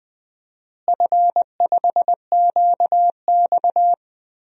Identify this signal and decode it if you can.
Morse code, heard as a keyed tone, about 21 wpm, 700 Hz. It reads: F5QX
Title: DXCC-100 Call Signs - 4 Characters - Repeated 2x - 20wpm